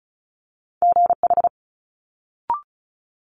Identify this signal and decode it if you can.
Morse code, heard as a keyed tone, about 36 wpm, 700 Hz. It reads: GH
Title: Multi-letter Phonemes Quick Fire 35wpm